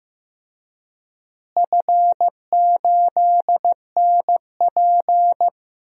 Morse code, heard as a keyed tone, about 15 wpm, 700 Hz. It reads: F8NP
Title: DXCC-100 Call Signs - 4 Characters - Repeated 3x - 15wpm